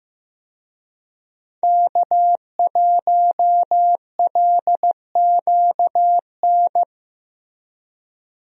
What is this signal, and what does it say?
Morse code, heard as a keyed tone, about 15 wpm, 700 Hz. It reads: K1LQN
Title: USA Call Signs - 5 Characters - Repeated 2x - 15wpm